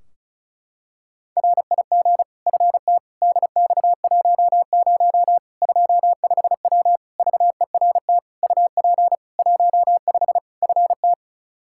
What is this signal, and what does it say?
Morse code, heard as a keyed tone, about 36 wpm, 700 Hz. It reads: RIG FT DX10 25W VERT UP 15 FT
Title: K1USN SST QSO Practice 35wpm